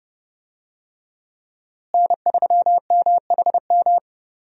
Morse code, heard as a keyed tone, about 30 wpm, 700 Hz. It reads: N3MHM
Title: USA Call Signs - 5 Characters - Repeated 3x - 30wpm